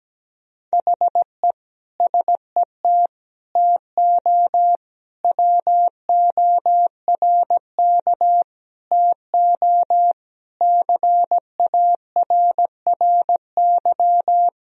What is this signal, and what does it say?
Morse code, heard as a keyed tone, about 17 wpm, 700 Hz. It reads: HE SET TO WORK TO CARRY
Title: Sentences from Top 1000 Words - Review Part 1 - 17wpm